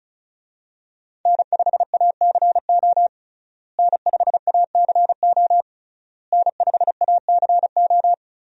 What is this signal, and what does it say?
Morse code, heard as a keyed tone, about 34 wpm, 700 Hz. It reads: N5ACO N5ACO N5ACO
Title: USA Call Signs - 5 Characters - Repeated 3x - 35wpm